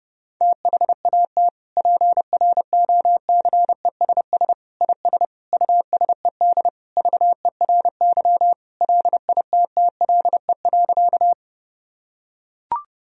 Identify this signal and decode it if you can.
Morse code, heard as a keyed tone, about 30 wpm, 700 Hz. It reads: THAT PROCESS IS USED VERY LITTLE.
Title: Sentences from Top 1000 Words - Review Part 3 - 30wpm